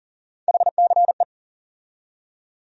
Morse code, heard as a keyed tone, about 40 wpm, 700 Hz. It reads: HCE
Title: USA Call Sign Postfixes - 3 Letters 40wpm